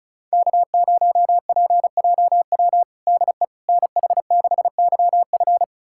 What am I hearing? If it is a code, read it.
Morse code, heard as a keyed tone, about 36 wpm, 700 Hz. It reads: K0PJW DE NH6YF
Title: USA Callsign DE Callsign - 5 Characters - 35wpm